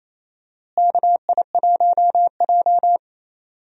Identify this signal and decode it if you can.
Morse code, heard as a keyed tone, about 28 wpm, 700 Hz. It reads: KI1J
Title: USA Call Signs - 4 Characters - Repeated 2x - 28wpm